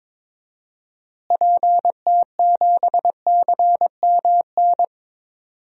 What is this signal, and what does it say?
Morse code, heard as a keyed tone, about 22 wpm, 700 Hz. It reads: PT7CMN
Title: DXCC-100 Call Signs - 6 Characters - Repeated 2x - 22wpm